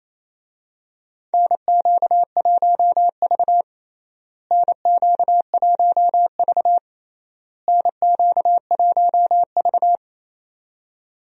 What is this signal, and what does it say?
Morse code, heard as a keyed tone, about 28 wpm, 700 Hz. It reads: NQ1V NQ1V NQ1V
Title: USA Call Signs - 4 Characters - Repeated 3x - 28wpm